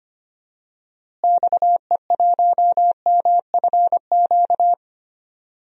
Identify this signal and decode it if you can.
Morse code, heard as a keyed tone, about 25 wpm, 700 Hz. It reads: XE1MFQ
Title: DXCC-100 Call Signs - 6 Characters - Repeated 2x - 25wpm